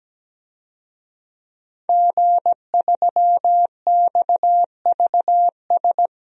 Morse code, heard as a keyed tone, about 17 wpm, 700 Hz. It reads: G3XVS
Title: DXCC-100 Call Signs - 5 Characters - Repeated 2x - 17wpm